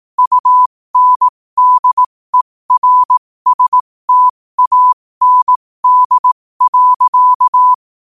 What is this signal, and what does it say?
Morse code, heard as a keyed tone, about 18 wpm, spaced as for 15 wpm, 1000 Hz. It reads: UNDERSTAND.
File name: qotdfiles / 15wpm